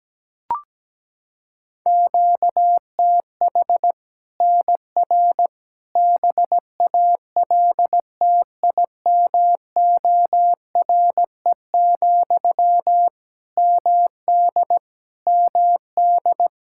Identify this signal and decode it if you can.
Morse code, heard as a keyed tone, about 17 wpm, 700 Hz. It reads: QTH NR BALTIMORE, MD MD
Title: Practice QSO - Exchange 1 - Beginning - 17wpm